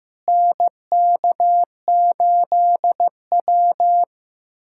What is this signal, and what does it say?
Morse code, heard as a keyed tone, about 15 wpm, 700 Hz. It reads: NK8W
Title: USA Callsign DE Callsign - 4 Characters - 15wpm